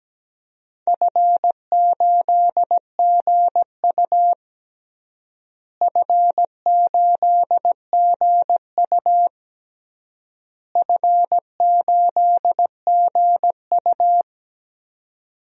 Morse code, heard as a keyed tone, about 17 wpm, 700 Hz. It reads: F8GU F8GU F8GU
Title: DXCC-100 Call Signs - 4 Characters - Repeated 3x - 17wpm